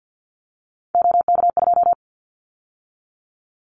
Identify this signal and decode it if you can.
Morse code, heard as a keyed tone, about 50 wpm, 700 Hz. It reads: OK2
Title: Callsign Prefixes for Top 50 DXCC - 50wpm